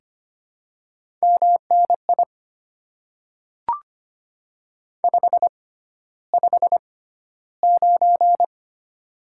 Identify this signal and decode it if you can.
Morse code, heard as a keyed tone, about 25 wpm, spaced as for 13 wpm, 700 Hz. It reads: MNI 559
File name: Top 90 QSO Elements - 6x Character Spacing - 25wpm